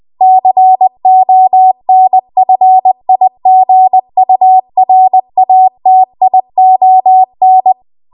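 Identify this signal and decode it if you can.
Morse code, heard as a keyed tone, about 20 wpm, 750 Hz. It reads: CONFIGURATION
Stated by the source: Morse code practice at 20 WPM.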